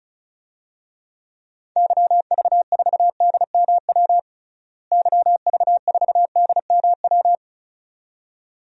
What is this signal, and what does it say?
Morse code, heard as a keyed tone, about 34 wpm, 700 Hz. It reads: YV4DMW YV4DMW
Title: DXCC-100 Call Signs - 6 Characters - Repeated 2x - 35wpm